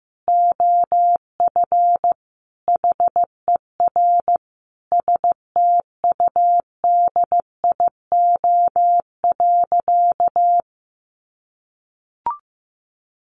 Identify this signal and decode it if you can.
Morse code, heard as a keyed tone, about 15 wpm, 700 Hz. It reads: OF HER STUDIO.
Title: Sentences from Top 1200 Words 15wpm